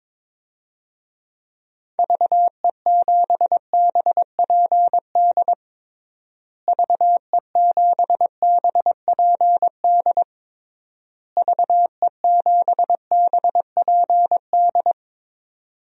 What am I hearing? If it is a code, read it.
Morse code, heard as a keyed tone, about 22 wpm, 700 Hz. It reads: VE7BPD VE7BPD VE7BPD
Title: DXCC-100 Call Signs - 6 Characters - Repeated 3x - 22wpm